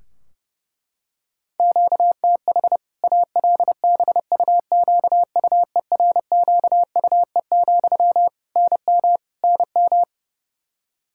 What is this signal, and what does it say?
Morse code, heard as a keyed tone, about 30 wpm, 700 Hz. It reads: QTH ALBUQUERQUE, NM NM
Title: Practice QSO - Exchange 1 - Beginning - 30wpm